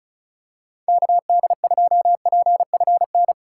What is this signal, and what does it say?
Morse code, heard as a keyed tone, about 36 wpm, 700 Hz. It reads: KD2PFN
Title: USA Call Signs - 3x Character Spacing - 35wpm